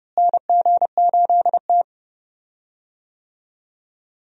Morse code, heard as a keyed tone, about 32 wpm, 700 Hz. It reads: NG8T
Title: USA Call Signs - 4 Characters - Repeated 2x - 30wpm